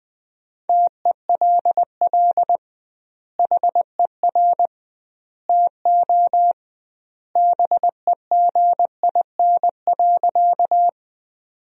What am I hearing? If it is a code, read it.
Morse code, heard as a keyed tone, about 20 wpm, 700 Hz. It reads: TELL HER TO BEGIN.
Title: 2x Word-Spacing - Sentences from Top 300 Words - 20wpm